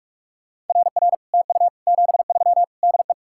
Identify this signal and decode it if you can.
Morse code, heard as a keyed tone, about 49 wpm, 700 Hz. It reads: RR TU 73 DE